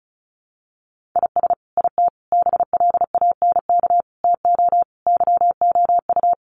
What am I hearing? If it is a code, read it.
Morse code, heard as a keyed tone, about 34 wpm, 700 Hz. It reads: IS IT BLANK TO YOU
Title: Sentences from Top 1500 Words 35wpm